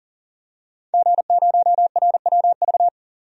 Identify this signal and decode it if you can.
Morse code, heard as a keyed tone, about 40 wpm, 700 Hz. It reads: G0RWV
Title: DXCC-100 Call Signs - 5 Characters - 40wpm